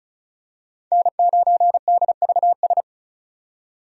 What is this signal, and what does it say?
Morse code, heard as a keyed tone, about 34 wpm, 700 Hz. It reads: N9DVS
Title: USA Call Signs - 5 Characters - 35wpm